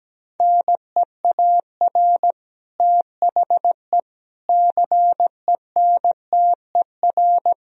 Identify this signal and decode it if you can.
Morse code, heard as a keyed tone, about 17 wpm, 700 Hz. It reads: NEAR THE CENTER
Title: Sentences from Top 1000 Words - Review Part 2 - 17wpm